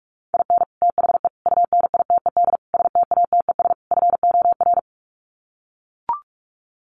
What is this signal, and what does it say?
Morse code, heard as a keyed tone, about 44 wpm, 700 Hz. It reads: IN THE UNITED STATES FOR
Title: Sets of 5 Words Encore2 45wpm